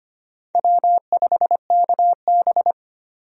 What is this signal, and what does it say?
Morse code, heard as a keyed tone, about 25 wpm, 700 Hz. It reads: W5KB
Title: USA Call Signs - 4 Characters - 25wpm